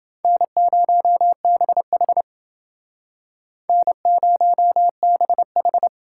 Morse code, heard as a keyed tone, about 29 wpm, 700 Hz. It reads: N0BH N0BH